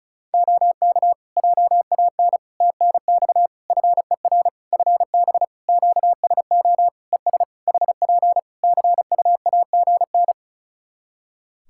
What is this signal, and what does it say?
Morse code, heard as a keyed tone, about 36 wpm, 700 Hz. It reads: OK JAN TNX FER FB QSO ES HP CUAGN
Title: K1USN SST QSO Practice 35wpm